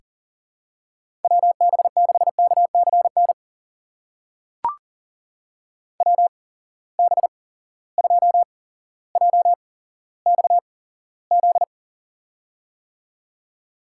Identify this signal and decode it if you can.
Morse code, heard as a keyed tone, about 40 wpm, spaced as for 17 wpm, 700 Hz. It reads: WB6KCN WB2JXZ
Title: USA Call Signs - 6 Characters - 8x Character Spacing - 40wpm